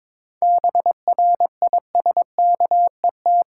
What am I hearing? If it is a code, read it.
Morse code, heard as a keyed tone, about 22 wpm, 700 Hz. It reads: BRISKET
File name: practice / 22wpm / Difficult Words with Hints - 22wpm